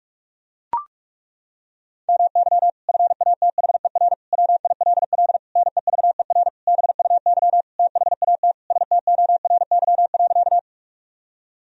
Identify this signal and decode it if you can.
Morse code, heard as a keyed tone, about 41 wpm, 700 Hz. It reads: MY FATHER WILL NEVER BUY THAT STORY.
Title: Sentences from Top 1000 Words - Review Part 4 - 45wpm